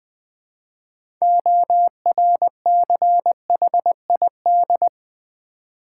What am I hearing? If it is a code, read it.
Morse code, heard as a keyed tone, about 20 wpm, 690 Hz. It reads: ORCHID
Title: Easy Words with Hints - 20wpm